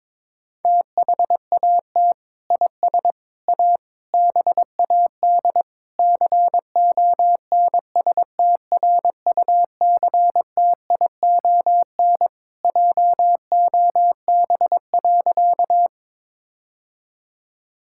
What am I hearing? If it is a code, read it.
Morse code, heard as a keyed tone, about 22 wpm, 700 Hz. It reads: THAT IS A BAD CONSTRUCTION JOB.
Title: Sentences from Top 1600 Words 22wpm